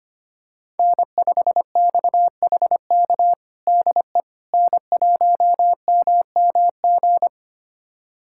Morse code, heard as a keyed tone, about 25 wpm, 700 Hz. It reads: N5XHK DE N1MMG
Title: USA Callsign DE Callsign - 5 Characters - 25wpm